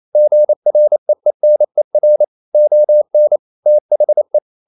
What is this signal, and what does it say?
Morse code, heard as a keyed tone, about 28 wpm, 600 Hz. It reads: GREENER ON THE
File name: CW / Practice / EnglishSayings28